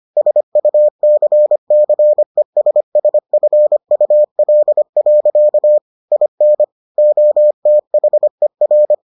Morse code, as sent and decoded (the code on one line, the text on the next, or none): ... ..- -.-. -.-. ... ... ..-. ..- .-.. .-.-.- / .. -. / --- - .... .-.
SUCCESSFUL. IN OTHER